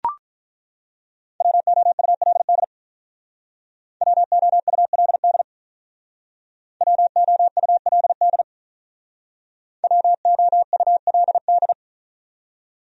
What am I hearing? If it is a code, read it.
Morse code, heard as a keyed tone, about 42 wpm, 700 Hz. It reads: WOULD WOULD WOULD WOULD
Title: Speed-Racing Top 500 Words 35wpm